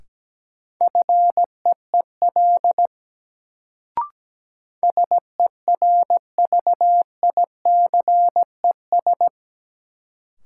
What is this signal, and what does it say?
Morse code, heard as a keyed tone, about 17 wpm, 700 Hz. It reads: FEEL SERVICES
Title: Sets of 5 Words - New Word Drill - 17wpm